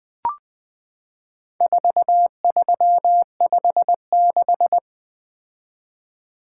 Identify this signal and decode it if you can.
Morse code, heard as a keyed tone, about 20 wpm, 700 Hz. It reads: 4356
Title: Four Letter-Number Combinations - 20wpm